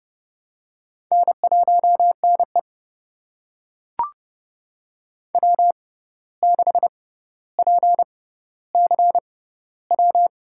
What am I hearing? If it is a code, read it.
Morse code, heard as a keyed tone, about 30 wpm, spaced as for 14 wpm, 700 Hz. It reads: N1NE W6PCW
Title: USA Call Signs - 6x Character Spacing - 30wpm